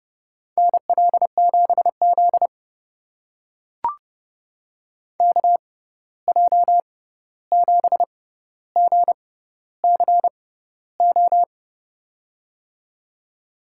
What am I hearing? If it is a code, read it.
Morse code, heard as a keyed tone, about 31 wpm, spaced as for 14 wpm, 700 Hz. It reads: NL7Z KJ7GCO
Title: USA Call Signs - 6x Character Spacing - 30wpm